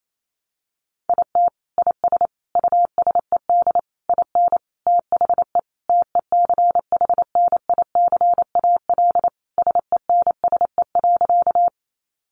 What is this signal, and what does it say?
Morse code, heard as a keyed tone, about 28 wpm, 700 Hz. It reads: IT IS USED IN THE TECHNICAL SENSE.
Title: Sentences from Top 1100 Words 28wpm